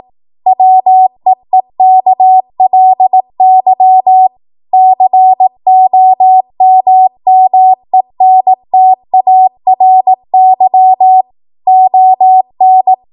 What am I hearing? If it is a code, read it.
Morse code, heard as a keyed tone, about 18 wpm, 750 Hz. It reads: WEEKLY COMMENTARY ON
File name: Morse / 18WPM